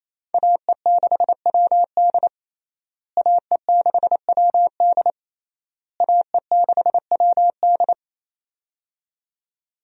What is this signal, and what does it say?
Morse code, heard as a keyed tone, about 28 wpm, 700 Hz. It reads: AE6WD AE6WD AE6WD
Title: USA Call Signs - 5 Characters - Repeated 3x - 28wpm